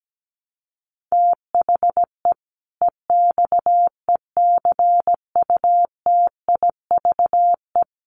Morse code, heard as a keyed tone, about 17 wpm, 700 Hz. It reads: THE EXECUTIVE
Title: Sets of 5 Words Encore 17wpm